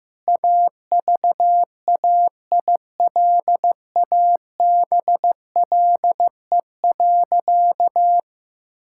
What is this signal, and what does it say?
Morse code, heard as a keyed tone, about 15 wpm, 700 Hz. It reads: AVAILABLE.
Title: Sentences from Top 1000 Words - Review Part 5 - 15wpm